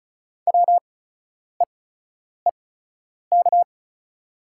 Morse code, heard as a keyed tone, about 36 wpm, spaced as for 10 wpm, 700 Hz. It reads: WEEK
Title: 8x Character Spacing - Top 500 Words - 35wpm